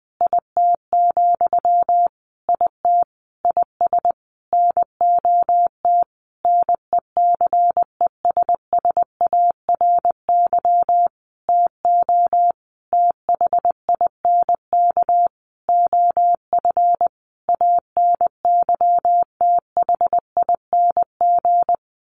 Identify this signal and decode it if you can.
Morse code, heard as a keyed tone, about 20 wpm, 700 Hz. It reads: IT, IT IS NOT NECESSARY TO THINK OF ANYTHING
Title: Sentences from Top 1300 Words 20wpm